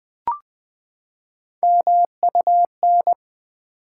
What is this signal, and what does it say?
Morse code, heard as a keyed tone, about 20 wpm, 700 Hz. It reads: MUN
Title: Build A Word - 3 Letters At A Time - 9-Letter Words - 20wpm